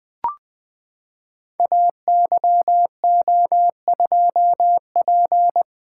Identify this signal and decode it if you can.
Morse code, heard as a keyed tone, about 20 wpm, 700 Hz. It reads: AYO2P
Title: Code Groups - 5 Letter:Numbers - 20wpm